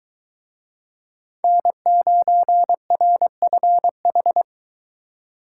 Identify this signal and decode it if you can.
Morse code, heard as a keyed tone, about 22 wpm, 700 Hz. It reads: N9RFH